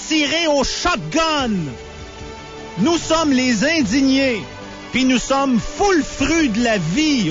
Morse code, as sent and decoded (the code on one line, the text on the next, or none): none